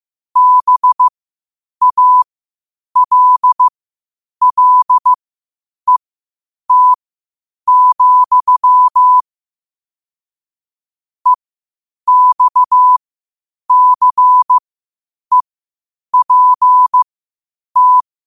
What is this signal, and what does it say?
Morse code, heard as a keyed tone, about 15 wpm, spaced as for 8 wpm, 1000 Hz. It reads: BALLET, EXCEPT
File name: qotdfiles / 8wpm